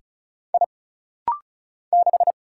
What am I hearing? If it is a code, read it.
Morse code, heard as a keyed tone, about 36 wpm, 700 Hz. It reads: I 6
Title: Single Letter-Number - Rapid-Fire 35wpm